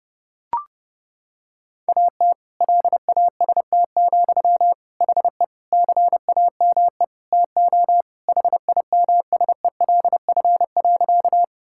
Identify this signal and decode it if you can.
Morse code, heard as a keyed tone, about 30 wpm, 700 Hz. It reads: AT LAST, HE CAME TO HIMSELF.
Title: Sentences from Top 1600 Words 30wpm